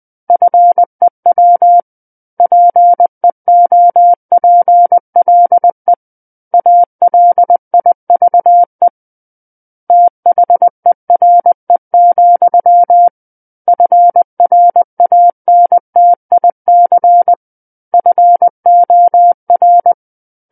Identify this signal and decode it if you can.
Morse code, heard as a keyed tone, about 20 wpm, 700 Hz. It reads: FEW PEOPLE ALIVE THERE, FRANTIC FOR
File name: 20 WPM